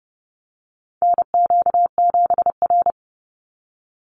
Morse code, heard as a keyed tone, about 30 wpm, 700 Hz. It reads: NQ7R
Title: NA SOTA Chaser Callsigns 30wpm